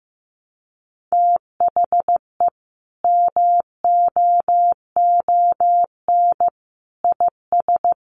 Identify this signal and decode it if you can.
Morse code, heard as a keyed tone, about 15 wpm, 700 Hz. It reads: THE MOON IS
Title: Sentences from Top 1200 Words 15wpm